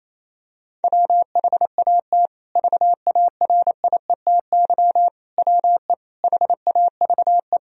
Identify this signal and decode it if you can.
Morse code, heard as a keyed tone, about 28 wpm, 700 Hz. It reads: WHAT VARIETY WE HAVE
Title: Sentences from Top 1000 Words - Review Part 4 - 28wpm